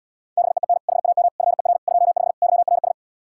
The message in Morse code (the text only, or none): BEEN HAM FER 25 YRS